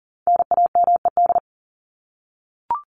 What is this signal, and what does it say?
Morse code, heard as a keyed tone, about 40 wpm, 700 Hz. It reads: NAMED
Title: ED Suffix - Top 100 Words with Base Words 40wpm